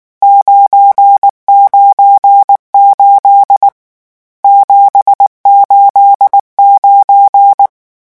Morse code, heard as a keyed tone, about 19 wpm, 800 Hz. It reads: 998 789
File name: morse mp3 / lecon12-vitesse 18